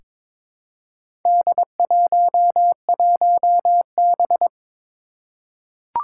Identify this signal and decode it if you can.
Morse code, heard as a keyed tone, about 22 wpm, 700 Hz. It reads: D11B
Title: Four Challenging Letter-Number Combinations 22wpm